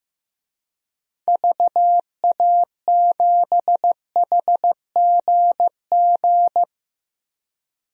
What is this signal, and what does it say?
Morse code, heard as a keyed tone, about 15 wpm, 700 Hz. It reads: VA7HGG
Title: DXCC-100 Call Signs - 6 Characters - 15wpm